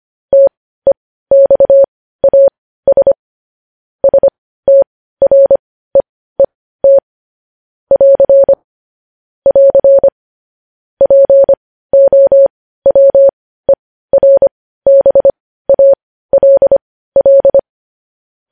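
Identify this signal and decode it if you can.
Morse code, heard as a keyed tone, about 25 wpm, spaced as for 15 wpm, 550 Hz. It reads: TEXAS STREET <AR> <AR> POWERBALL
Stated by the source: Morse code news headlines sent at 15 WPM.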